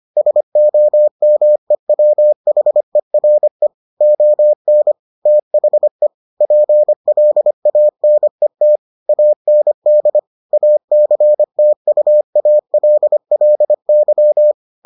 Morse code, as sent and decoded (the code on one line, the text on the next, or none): ... --- -- . .-- .... .-. / --- -. / - .... / .--. .-.. .- -. - / .- -. -.. / .- -.-. - ..- .- .-.. .-.. -.--
SOMEWHERE ON THE PLANET AND ACTUALLY